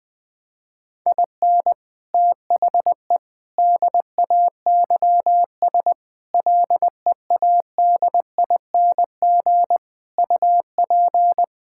Morse code, heard as a keyed tone, about 20 wpm, 700 Hz. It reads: IN THE DAYS LEADING UP
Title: Sentences with 6-Word Phrases - 20wpm